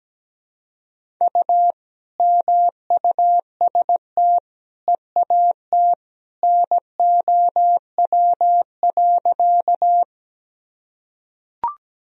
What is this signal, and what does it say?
Morse code, heard as a keyed tone, about 17 wpm, 700 Hz. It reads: U MUST EAT NOW.
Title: Sentences from Top 300 Words and Abbreviations 17wpm